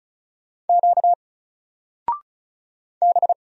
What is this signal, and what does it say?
Morse code, heard as a keyed tone, about 36 wpm, 700 Hz. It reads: Q B
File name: Single-Letter-Number-35wpm